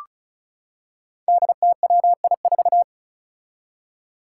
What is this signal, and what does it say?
Morse code, heard as a keyed tone, about 36 wpm, 700 Hz. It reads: DTWI4
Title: Five Challenging Letter-Number Combinations 35wpm